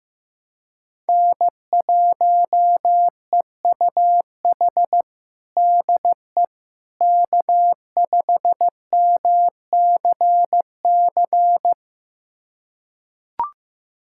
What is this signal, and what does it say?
Morse code, heard as a keyed tone, about 15 wpm, 700 Hz. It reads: N1EUH DE K5MCC
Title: USA Callsign DE Callsign - 5 Characters - 15wpm